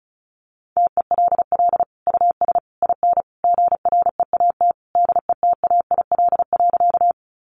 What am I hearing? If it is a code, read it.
Morse code, heard as a keyed tone, about 34 wpm, 700 Hz. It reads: TELL US IN GREAT DETAIL.
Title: Sentences from Top 1500 Words 35wpm